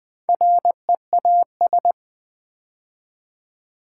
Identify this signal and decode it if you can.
Morse code, heard as a keyed tone, about 21 wpm, 700 Hz. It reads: REAS
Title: Build A Word - 4 Letters At A Time - 10-Letter Words - 20wpm